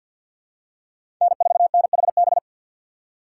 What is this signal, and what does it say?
Morse code, heard as a keyed tone, about 53 wpm, 700 Hz. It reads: N4NHB
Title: USA Call Signs - 5 Characters - 8x Character Spacing - 50wpm